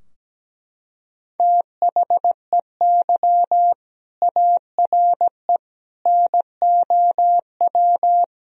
Morse code, heard as a keyed tone, about 17 wpm, 700 Hz. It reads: THEY ARE NOW
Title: Sentences from Top 1000 Words - Review Part 2 - 17wpm